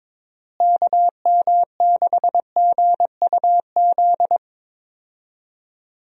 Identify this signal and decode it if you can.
Morse code, heard as a keyed tone, about 22 wpm, 700 Hz. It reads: KM6GUZ
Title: USA Call Signs - 6 Characters - Repeated 2x - 22wpm